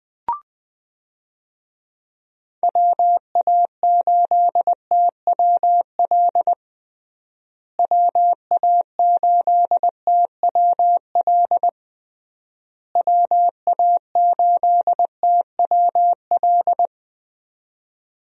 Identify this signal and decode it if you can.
Morse code, heard as a keyed tone, about 21 wpm, 700 Hz. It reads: WA8TWL WA8TWL WA8TWL
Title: USA Call Signs - 6 Characters - Repeated 3x - 20wpm